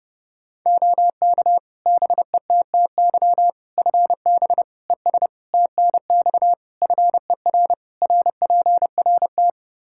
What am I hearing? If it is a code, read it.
Morse code, heard as a keyed tone, about 30 wpm, 700 Hz. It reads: OK BETTY FB ES TNX FER RPRT
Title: Practice QSO - Exchange 2 - Info - 30wpm